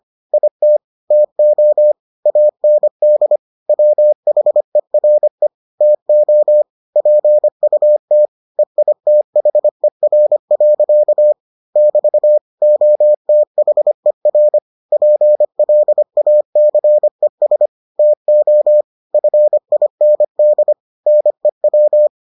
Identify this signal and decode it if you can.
Morse code, heard as a keyed tone, about 25 wpm, 600 Hz. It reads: IT TO AND WHERE TO PUT EITHER. <BT> OTHER PLACES TO FIND NEW